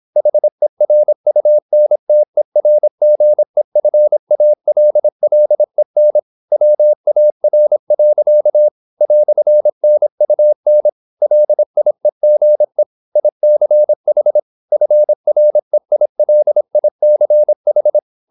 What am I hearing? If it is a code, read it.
Morse code, heard as a keyed tone, about 26 wpm, 600 Hz. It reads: HERUNTERGEFALLEN WAR. "NUN LIEGE ICH FREILICH